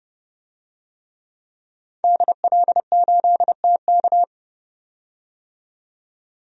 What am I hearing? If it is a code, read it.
Morse code, heard as a keyed tone, about 31 wpm, 700 Hz. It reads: DL8TK
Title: DXCC-100 Call Signs - 5 Characters - 30wpm